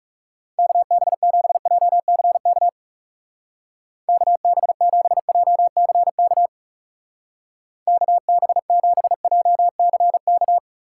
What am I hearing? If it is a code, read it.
Morse code, heard as a keyed tone, about 36 wpm, 700 Hz. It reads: KB7JCK KB7JCK KB7JCK
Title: Speed Racing - USA 2x3 Call Signs 30wpm